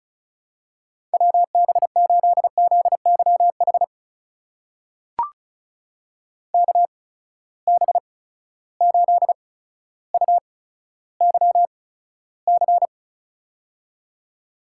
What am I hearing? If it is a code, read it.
Morse code, heard as a keyed tone, about 34 wpm, spaced as for 15 wpm, 700 Hz. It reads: WB8ZYH KB8UYC
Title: USA Call Signs - 6 Characters - 8x Character Spacing - 35wpm